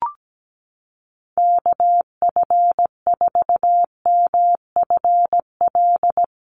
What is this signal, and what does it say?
Morse code, heard as a keyed tone, about 17 wpm, 700 Hz. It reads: KF4MFL
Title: USA Call Signs Repeated 2x 17wpm